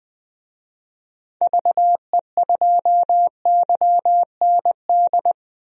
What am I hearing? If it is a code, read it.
Morse code, heard as a keyed tone, about 20 wpm, 700 Hz. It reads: VE2YND
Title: DXCC-100 Call Signs - 6 Characters - Repeated 3x - 20wpm